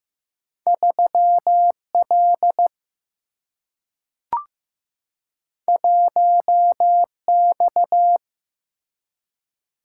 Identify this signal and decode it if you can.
Morse code, heard as a keyed tone, about 15 wpm, 700 Hz. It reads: 3 L 1 X
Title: Single-Letter Single-Number 15wpm